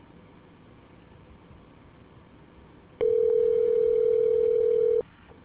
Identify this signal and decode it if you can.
Ringback tone (North America), 440 + 480 Hz.